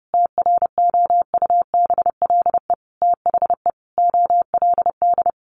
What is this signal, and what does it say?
Morse code, heard as a keyed tone, about 30 wpm, 700 Hz. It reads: TROUBLE THE OLD
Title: Sentences from Top 1200 Words 30wpm